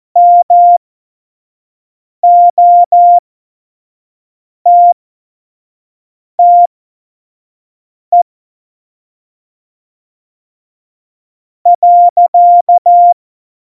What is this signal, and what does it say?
Morse code, heard as a keyed tone, about 14 wpm, spaced as for 4 wpm, 700 Hz. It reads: MOTTE.
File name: Morsekurs / Witze / Witz 25. Dezember 2018, 25 F70